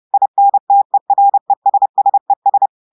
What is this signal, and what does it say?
Morse code, heard as a keyed tone, about 30 wpm, 800 Hz. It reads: INTERESSES
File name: audios / portugais / portugais texte en clair 1 vitesse 30 mots minutes